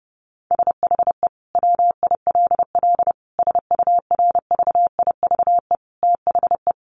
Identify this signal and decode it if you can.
Morse code, heard as a keyed tone, about 28 wpm, 700 Hz. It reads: SHE WILL SURVIVE THE
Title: Sentences from Top 1300 Words 30wpm